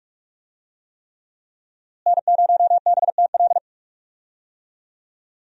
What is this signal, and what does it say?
Morse code, heard as a keyed tone, about 41 wpm, 700 Hz. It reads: N0BTL
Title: USA Call Signs - 5 Characters - Repeated 3x - 45wpm